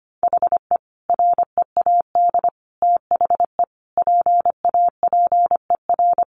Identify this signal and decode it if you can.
Morse code, heard as a keyed tone, about 25 wpm, 700 Hz. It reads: HE READ THE PAPER
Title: Sentences from Top 1100 Words 25wpm